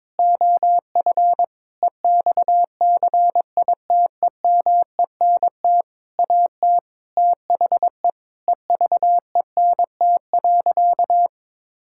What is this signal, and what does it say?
Morse code, heard as a keyed tone, about 22 wpm, 700 Hz. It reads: OF EXCITEMENT AT THE EVENT.
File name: Sentences with 7-Word Phrases - 22wpm